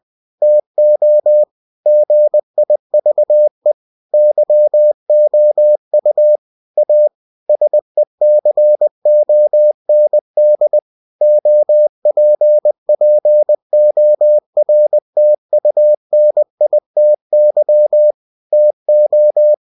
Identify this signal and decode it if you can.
Morse code, heard as a keyed tone, about 20 wpm, 600 Hz. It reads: TO GIVE YOU A SECOND OPPORTUNITY TO
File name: cw-books / winnie-the-pooh / winnie-20wpm-page-0119